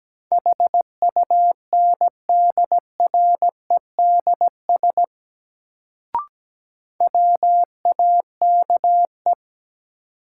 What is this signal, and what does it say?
Morse code, heard as a keyed tone, about 17 wpm, 700 Hz. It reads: HUNDREDS WAKE
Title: Sets of 4 Words - New Word Drill - 17wpm